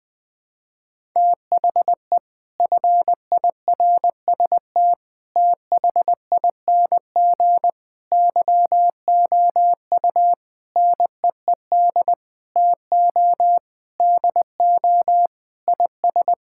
Morse code, heard as a keyed tone, about 20 wpm, 700 Hz. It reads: THE FIRST THING YOU NEED TO DO IS
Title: Sentences with 8-Word Phrases - 20wpm